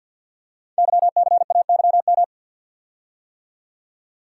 Morse code, heard as a keyed tone, about 48 wpm, 700 Hz. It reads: YCAYM
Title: Code Groups - 5 Letters - 50wpm